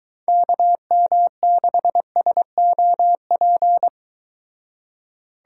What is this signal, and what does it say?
Morse code, heard as a keyed tone, about 23 wpm, 700 Hz. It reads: KM6SOP